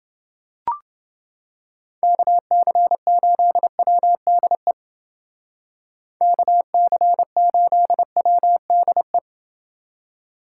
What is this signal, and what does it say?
Morse code, heard as a keyed tone, about 29 wpm, 700 Hz. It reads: KC8WDE KC8WDE